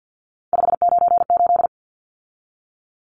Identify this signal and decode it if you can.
Morse code, heard as a keyed tone, about 50 wpm, 700 Hz. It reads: 598
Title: Numbers - 3-Digit 50wpm